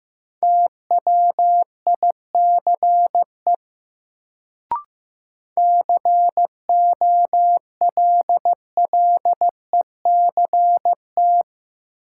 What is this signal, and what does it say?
Morse code, heard as a keyed tone, about 15 wpm, 700 Hz. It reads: TWICE COLLECT
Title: Sentences from Top 1600 Words - New Word Drill Alt - 15wpm